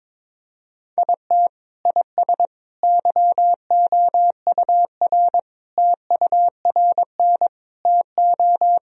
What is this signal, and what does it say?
Morse code, heard as a keyed tone, about 22 wpm, 700 Hz. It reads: IT IS YOUR TURN TO
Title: Sentences from Top 1000 Words - Review Part 2 - 22wpm